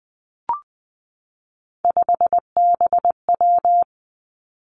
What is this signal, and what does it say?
Morse code, heard as a keyed tone, about 20 wpm, 700 Hz. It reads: 5BW
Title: Three Challenging Letter-Number Combinations 20wpm